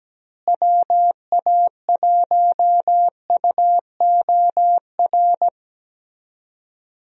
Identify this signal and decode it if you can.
Morse code, heard as a keyed tone, about 17 wpm, 700 Hz. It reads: WA1UOR
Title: USA Call Signs - 6 Characters - Repeated 3x - 17wpm